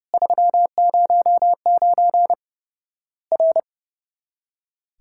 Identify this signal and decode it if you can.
Morse code, heard as a keyed tone, about 30 wpm, 700 Hz. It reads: 309 R
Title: Worked All Europe DX Contest - QTC Exchange - 30wpm